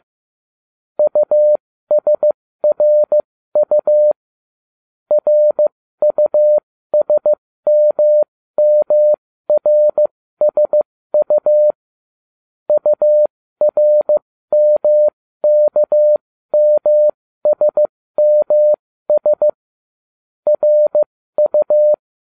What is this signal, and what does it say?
Morse code, heard as a keyed tone, about 15 wpm, spaced as for 13 wpm, 600 Hz. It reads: USRU RUSMMRSU URMKMSMS RU